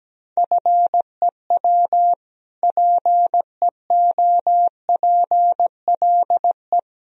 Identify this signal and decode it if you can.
Morse code, heard as a keyed tone, about 17 wpm, 700 Hz. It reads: FEW PEOPLE
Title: Sentences from Top 1000 Words - Review Part 1 - 17wpm